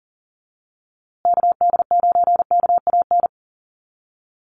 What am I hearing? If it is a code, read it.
Morse code, heard as a keyed tone, about 40 wpm, 700 Hz. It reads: KD9KAN
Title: USA Call Signs Repeated 3x 40wpm